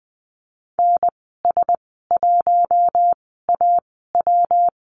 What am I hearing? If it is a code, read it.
Morse code, heard as a keyed tone, about 20 wpm, spaced as for 14 wpm, 700 Hz. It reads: NS1AW
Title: 140 Lesson 23g - Letter f - Call Signs 20wpm